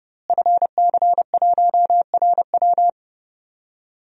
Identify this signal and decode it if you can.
Morse code, heard as a keyed tone, about 30 wpm, 700 Hz. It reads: FC1RW
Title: DXCC-100 Call Signs - 5 Characters - Repeated 3x - 30wpm